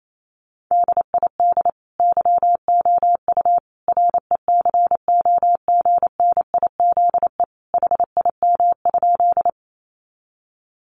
Morse code, heard as a keyed tone, about 28 wpm, 700 Hz. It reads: DID YOU RECOGNIZE HIM?
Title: Sentences from Top 1400 Words 28wpm